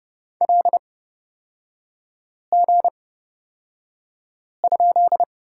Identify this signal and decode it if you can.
Morse code, heard as a keyed tone, about 30 wpm, spaced as for 5 wpm, 700 Hz. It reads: LG?
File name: Lesson 36e - BT - Three Characters 30wpm